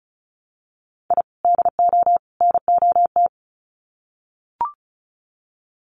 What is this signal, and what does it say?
Morse code, heard as a keyed tone, about 34 wpm, 700 Hz. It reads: I DO NOT
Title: Sets of 3 Words 35wpm